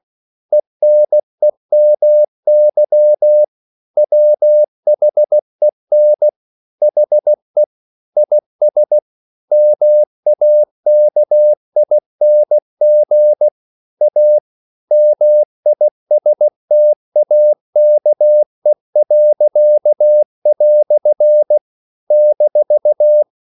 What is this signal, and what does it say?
Morse code, heard as a keyed tone, about 16 wpm, 600 Hz. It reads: ENEMY WHEN HE IS MAKING A MISTAKE." -
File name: CW / Practice / EnglishQuotations0116